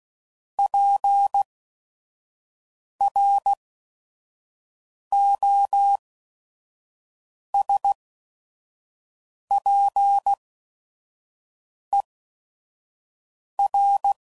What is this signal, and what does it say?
Morse code, heard as a keyed tone, about 16 wpm, spaced as for 5 wpm, 800 Hz. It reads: PROSPER
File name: CW Code Groups @5WPM / fromSM's / Cities